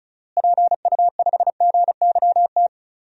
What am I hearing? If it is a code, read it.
Morse code, heard as a keyed tone, about 34 wpm, 700 Hz. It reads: PU5GYT
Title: DXCC-100 Call Signs - 6 Characters - Repeated 2x - 35wpm